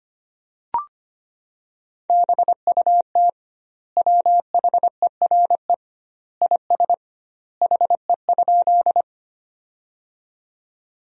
Morse code, heard as a keyed tone, about 25 wpm, 700 Hz. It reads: BUT WHERE IS HE?
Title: 2x Word-Spacing - Sentences from Top 200 Words - 25wpm